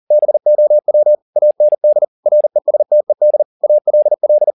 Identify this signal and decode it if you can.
Morse code, heard as a keyed tone, about 40 wpm, 600 Hz. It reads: BOW AND RESTED ALL
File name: the old man and the sea 2 40wpm